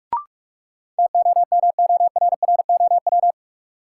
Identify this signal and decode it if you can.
Morse code, heard as a keyed tone, about 42 wpm, 700 Hz. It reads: TOMORROW
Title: Sentences from Top 1100 Words - New Word Drill Alt - 45wpm